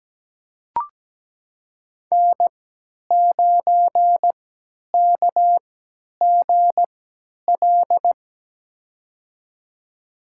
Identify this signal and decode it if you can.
Morse code, heard as a keyed tone, about 17 wpm, spaced as for 10 wpm, 700 Hz. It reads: N9KGL
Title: USA Call Signs - 5 Characters - 3x Character Spacing - 17wpm